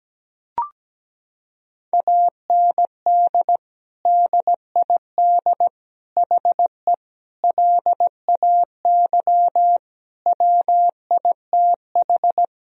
Morse code, heard as a keyed tone, about 17 wpm, 700 Hz. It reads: AND DID HE LAY WITH
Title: Sentences from Top 1000 Words - Review Part 2 - 17wpm